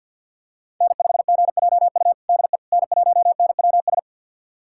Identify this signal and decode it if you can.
Morse code, heard as a keyed tone, about 51 wpm, 700 Hz. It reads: N5GJU DE N1NWS
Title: USA Callsign DE Callsign - 5 Characters - 50wpm